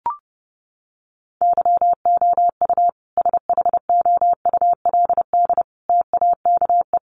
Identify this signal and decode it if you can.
Morse code, heard as a keyed tone, about 30 wpm, 700 Hz. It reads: YOU SHOULD TAKE
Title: Sentences from Top 300 Words 30wpm